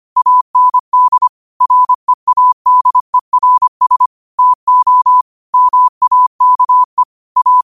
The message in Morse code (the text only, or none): AND READERS TO MAKE A